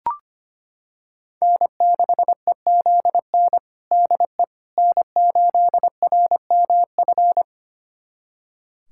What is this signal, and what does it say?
Morse code, heard as a keyed tone, about 25 wpm, 700 Hz. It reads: N6EZN DE N8RMF
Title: USA Callsign DE Callsign - 5 Characters - 25wpm